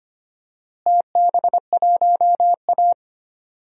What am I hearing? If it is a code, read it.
Morse code, heard as a keyed tone, about 25 wpm, 700 Hz. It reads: TB1A
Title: DXCC-100 Call Signs - 4 Characters - Repeated 3x - 25wpm